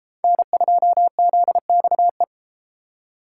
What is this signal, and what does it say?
Morse code, heard as a keyed tone, about 31 wpm, 700 Hz. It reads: N2ZXE